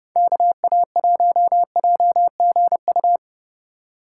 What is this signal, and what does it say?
Morse code, heard as a keyed tone, about 30 wpm, 700 Hz. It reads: KA1JGU